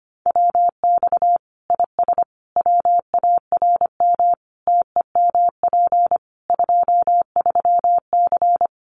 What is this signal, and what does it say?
Morse code, heard as a keyed tone, about 25 wpm, 700 Hz. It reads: WX IS WARM TEMP 23C
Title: Common QSO Phrases 25wpm